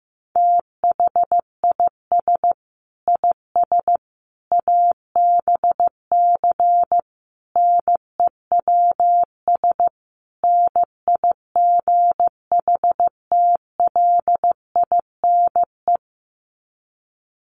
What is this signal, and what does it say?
Morse code, heard as a keyed tone, about 15 wpm, 700 Hz. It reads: THIS IS ABC NEWS NIGHTLINE
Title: Sets of 5 Words 15wpm